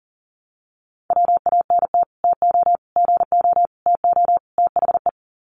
Morse code, heard as a keyed tone, about 40 wpm, 700 Hz. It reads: WANT TO GO TO THE
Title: Sets of 5 Words Encore2 40wpm